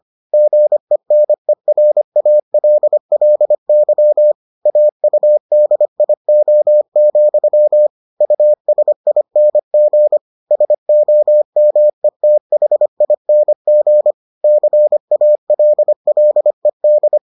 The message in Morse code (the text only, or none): GENERALLY AUDIO, USING SOMETHING CALLED